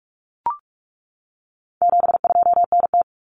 Morse code, heard as a keyed tone, about 44 wpm, 700 Hz. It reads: 72NT
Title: Common Cut Numbers - 4-Digit 45wpm